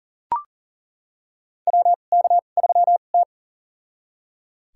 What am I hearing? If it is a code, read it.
Morse code, heard as a keyed tone, about 37 wpm, spaced as for 28 wpm, 700 Hz. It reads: WK3T
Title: USA Call Signs - 4 Characters - 2x Character Spacing - 40wpm